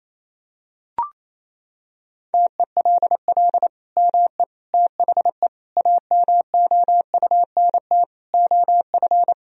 Morse code, heard as a keyed tone, about 28 wpm, 700 Hz. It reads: TELL ME THE AMOUNT OF
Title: Sentences from Top 1000 Words - Review Part 4 - 28wpm